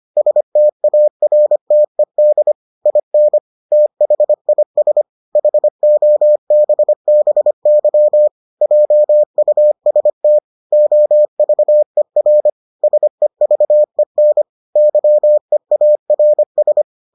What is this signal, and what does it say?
Morse code, heard as a keyed tone, about 25 wpm, 600 Hz. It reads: STARTED IN THIS HOBBY JUST OVER SEVEN YEARS